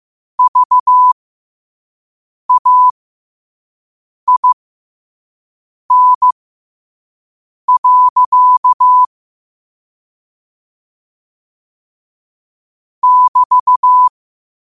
Morse code, heard as a keyed tone, about 15 wpm, spaced as for 5 wpm, 1000 Hz. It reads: VAIN. <BT>